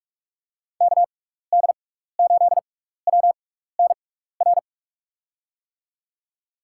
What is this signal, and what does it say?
Morse code, heard as a keyed tone, about 46 wpm, 700 Hz. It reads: K D 8 W N R
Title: USA Call Signs - 6 Characters - 6x Character Spacing - 45wpm